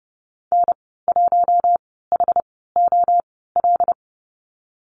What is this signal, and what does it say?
Morse code, heard as a keyed tone, about 30 wpm, spaced as for 17 wpm, 700 Hz. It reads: N1HOL
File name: Lesson 15g - Number 5 - Call Signs 30wpm